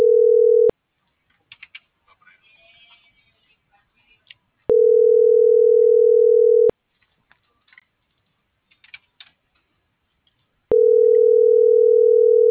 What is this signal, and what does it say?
Ringback tone (North America), 440 + 480 Hz.